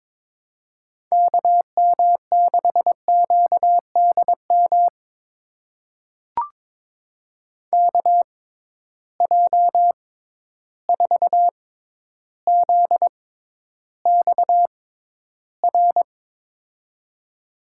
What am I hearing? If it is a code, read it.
Morse code, heard as a keyed tone, about 22 wpm, spaced as for 11 wpm, 700 Hz. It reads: KM6QDM KJ4ZXR
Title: USA Call Signs - 6 Characters - 6x Character Spacing - 22wpm